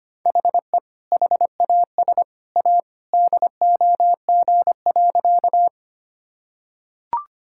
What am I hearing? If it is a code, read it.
Morse code, heard as a keyed tone, about 25 wpm, 700 Hz. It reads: HE HAS A DOG.
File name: practice / 25wpm / Sentences with 2-Word Phrases - 25wpm